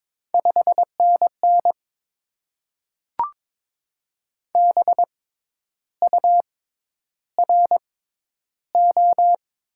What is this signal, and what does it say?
Morse code, heard as a keyed tone, about 22 wpm, spaced as for 10 wpm, 700 Hz. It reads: 5NN BURO
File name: Top 90 QSO Elements - 6x Character Spacing - 22wpm